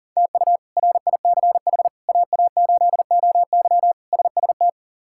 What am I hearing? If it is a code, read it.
Morse code, heard as a keyed tone, about 40 wpm, 700 Hz. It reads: TU RICH AA8OY SST
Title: K1USN SST QSO Practice - 40wpm